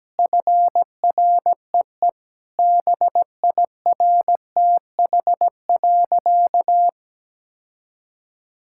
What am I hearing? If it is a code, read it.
Morse code, heard as a keyed tone, about 17 wpm, 700 Hz. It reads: FREE BIRTH.
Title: Sentences from Top 1000 Words - Review Part 5 - 17wpm